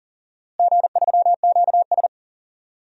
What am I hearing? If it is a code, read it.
Morse code, heard as a keyed tone, about 40 wpm, 700 Hz. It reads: G3QS
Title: DXCC-100 Call Signs - 4 Characters - Repeated 2x - 40wpm